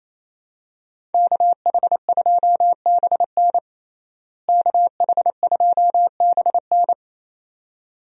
Morse code, heard as a keyed tone, about 28 wpm, 700 Hz. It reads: KH2BN KH2BN
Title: USA Call Signs - 5 Characters - Repeated 2x - 28wpm